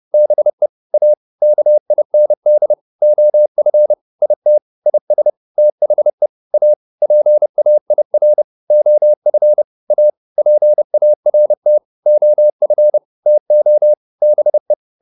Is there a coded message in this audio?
Yes: Morse code from 0.1 to 14.7 s, heard as a keyed tone, about 30 wpm, 600 Hz. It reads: BE A KIND OF IT IS THE A PAIR OF A PART OF TO BE